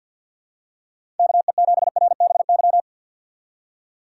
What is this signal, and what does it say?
Morse code, heard as a keyed tone, about 46 wpm, 700 Hz. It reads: KE7RBY
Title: USA Call Signs - 6 Characters - 3x Character Spacing - 50wpm